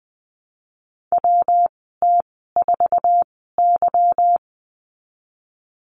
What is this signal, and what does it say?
Morse code, heard as a keyed tone, about 20 wpm, 700 Hz. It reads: W T 4 Y
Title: 154 Lesson 25g - Letter p - Call Signs 20wpm